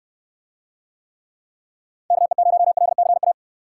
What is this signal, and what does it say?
Morse code, heard as a keyed tone, about 68 wpm, 700 Hz. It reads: KE0LCA